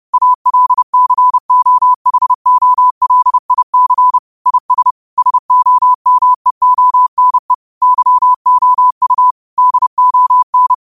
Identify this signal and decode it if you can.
Morse code, heard as a keyed tone, about 30 wpm, 1000 Hz. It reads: ALCOHOLIC IS SOMEONE YOU DON
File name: qotdfiles / 30wpm